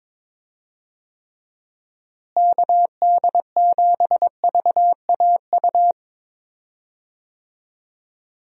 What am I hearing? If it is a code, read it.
Morse code, heard as a keyed tone, about 22 wpm, 700 Hz. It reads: KD7VAU
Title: USA Call Signs - 6 Characters - Repeated 2x - 22wpm